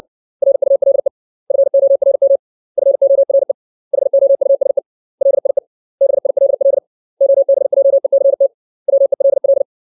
Morse code, heard as a keyed tone, about 60 wpm, 550 Hz. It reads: RULE FORM FORE HOUSE LIE BILL OBJECT WELL
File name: 632 words / 632 Words 60WPM